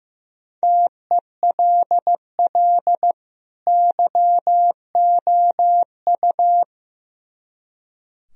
Morse code, heard as a keyed tone, about 17 wpm, 700 Hz. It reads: TELL YOU
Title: Speed-Racing - Sets of 2 Words 15wpm